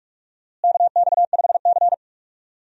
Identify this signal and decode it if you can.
Morse code, heard as a keyed tone, about 46 wpm, 700 Hz. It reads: KX5C
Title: USA Call Signs - 4 Characters - Repeated 2x - 45wpm